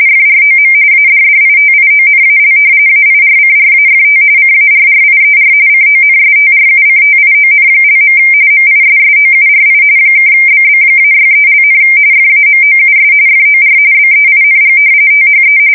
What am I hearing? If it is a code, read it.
RTTY, about 45.45 baud. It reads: FROM ADDITIONAL AIRCRAFT CARRIER CONSTRUCTION). MISSOURI (BB-63), FAMOUS FOR BEING THE
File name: fsk170 / battleships